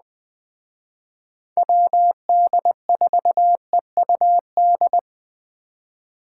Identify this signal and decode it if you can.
Morse code, heard as a keyed tone, about 21 wpm, 700 Hz. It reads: WD4EUD
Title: Speed Racing - USA 2x3 Call Signs 17wpm